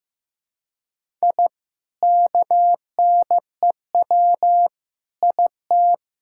Morse code, heard as a keyed tone, about 15 wpm, 700 Hz. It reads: I KNEW IT
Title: Sentences from Top 1000 Words - Review Part 6 - 15wpm